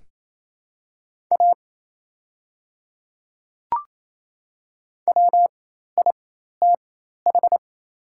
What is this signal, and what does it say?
Morse code, heard as a keyed tone, about 28 wpm, spaced as for 13 wpm, 700 Hz. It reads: A WITH
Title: 4x Character Spacing - Top 200 Words - 28wpm